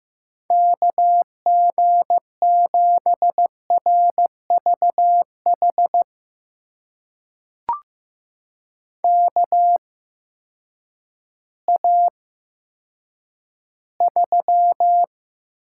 Morse code, heard as a keyed tone, about 15 wpm, spaced as for 9 wpm, 700 Hz. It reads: KG7RVH KA3
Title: USA Call Signs - 6 Characters - 8x Character Spacing - 15wpm